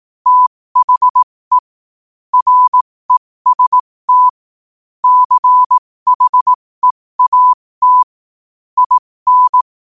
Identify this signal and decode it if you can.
Morse code, heard as a keyed tone, about 18 wpm, spaced as for 15 wpm, 1000 Hz. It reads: THE REST CHEAT IN
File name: qotdfiles / 15wpm